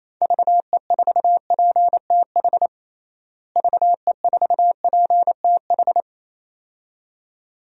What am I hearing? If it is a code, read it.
Morse code, heard as a keyed tone, about 28 wpm, 700 Hz. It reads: VE4PTH VE4PTH
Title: DXCC-100 Call Signs - 6 Characters - Repeated 2x - 28wpm